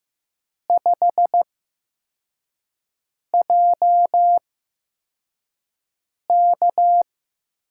Morse code, heard as a keyed tone, about 15 wpm, spaced as for 4 wpm, 700 Hz. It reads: 5JK
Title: USA Call Signs - 6 Characters - 8x Character Spacing - 15wpm